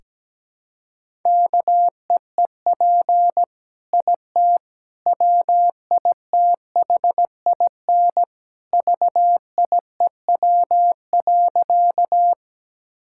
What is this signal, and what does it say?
Morse code, heard as a keyed tone, about 17 wpm, 700 Hz. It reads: KEEP IT WITHIN VIEW.
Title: Sentences from Top 1000 Words - Review Part 6 - 17wpm